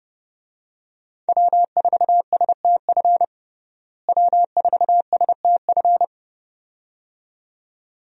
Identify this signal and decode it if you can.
Morse code, heard as a keyed tone, about 32 wpm, 700 Hz. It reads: W4STF W4STF
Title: USA Call Signs - 5 Characters - Repeated 2x - 30wpm